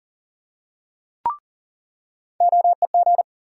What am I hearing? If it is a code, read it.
Morse code, heard as a keyed tone, about 40 wpm, 700 Hz. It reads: OEG
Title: USA Call Sign Postfixes - 3 Letters 40wpm